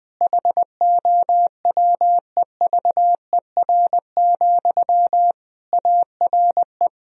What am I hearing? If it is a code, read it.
Morse code, heard as a keyed tone, about 20 wpm, 700 Hz. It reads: HOWEVER, ARE
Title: Sentences from Top 1000 Words - Review Part 1 - 20wpm